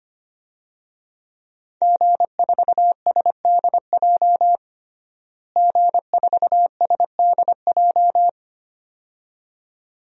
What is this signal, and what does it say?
Morse code, heard as a keyed tone, about 25 wpm, 700 Hz. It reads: G4SDJ G4SDJ
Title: DXCC-100 Call Signs - 5 Characters - Repeated 2x - 25wpm